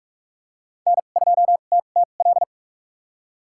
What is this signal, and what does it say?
Morse code, heard as a keyed tone, about 46 wpm, spaced as for 32 wpm, 700 Hz. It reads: N2TTL
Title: USA Call Signs - 5 Characters - 2x Character Spacing - 45wpm